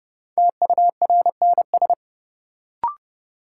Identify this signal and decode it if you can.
Morse code, heard as a keyed tone, about 30 wpm, 700 Hz. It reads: TURNS
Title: Sets of 5 Words Encore - New Word Drill - 30wpm